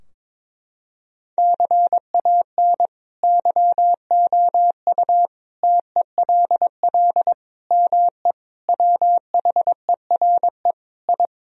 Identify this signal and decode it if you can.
Morse code, heard as a keyed tone, about 22 wpm, 700 Hz. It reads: CAN YOU TELL ME WHERE I
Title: Sentences from Top 1000 Words - Review Part 2 - 22wpm